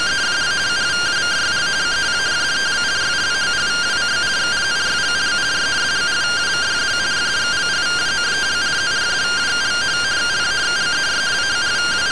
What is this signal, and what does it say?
Two-tone teleprinter signal, about 170 Hz shift.